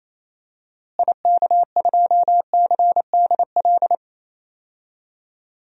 Morse code, heard as a keyed tone, about 28 wpm, 700 Hz. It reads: IK2CDL
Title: DXCC-100 Call Signs - 6 Characters - Repeated 3x - 28wpm